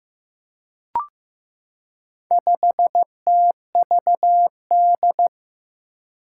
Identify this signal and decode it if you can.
Morse code, heard as a keyed tone, about 15 wpm, 700 Hz. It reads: 5TVD
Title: Four Challenging Letter-Number Combinations 15wpm